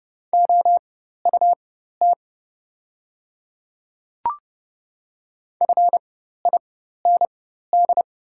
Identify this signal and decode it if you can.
Morse code, heard as a keyed tone, about 30 wpm, spaced as for 14 wpm, 690 Hz. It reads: OUT FIND
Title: 4x Character Spacing - Top 100 Words - 30wpm